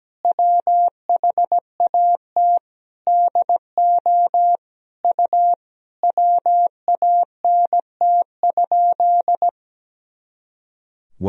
Morse code, heard as a keyed tone, about 17 wpm, 700 Hz. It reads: WHAT DO U WANT?
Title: Sentences from Top 200 Words and Abbreviations 17wpm